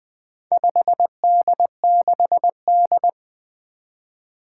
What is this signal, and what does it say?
Morse code, heard as a keyed tone, about 20 wpm, 700 Hz. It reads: 5D6D
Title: Four Challenging Letter-Number Combinations 20wpm